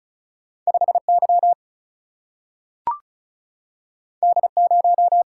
Morse code, heard as a keyed tone, about 36 wpm, 700 Hz. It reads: 5 Y D 0
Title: Single-Letter Single-Number 35wpm